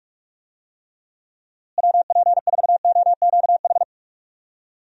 Morse code, heard as a keyed tone, about 46 wpm, 700 Hz. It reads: WP4OQH
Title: USA Call Signs - 6 Characters - Repeated 2x - 45wpm